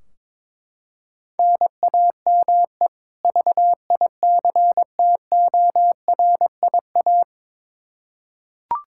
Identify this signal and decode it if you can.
Morse code, heard as a keyed tone, about 22 wpm, 700 Hz. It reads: NAME VICTORIA
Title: Practice QSO - Exchange 1 - Beginning - 22wpm